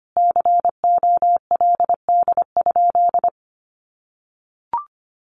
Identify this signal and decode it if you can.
Morse code, heard as a keyed tone, about 25 wpm, 700 Hz. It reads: COLD?
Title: Sentences from Top 1500 Words 25wpm